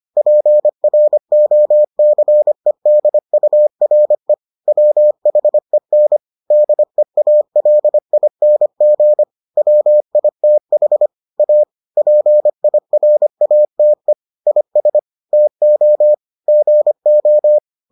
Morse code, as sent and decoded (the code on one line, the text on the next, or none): .--. .-. --- -.-. -.. ..- .-. / .-- .... -. / -.. .- .-.. .. -. --. / .-- .. - .... / .- / .--. .. .-. .- - . / .. ... / - --- / --. ---
PROCEDURE WHEN DEALING WITH A PIRATE IS TO GO